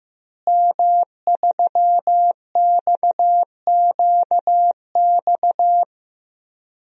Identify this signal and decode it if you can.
Morse code, heard as a keyed tone, about 16 wpm, 700 Hz. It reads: M3XQX
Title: DXCC-100 Call Signs - 5 Characters - Repeated 3x - 15wpm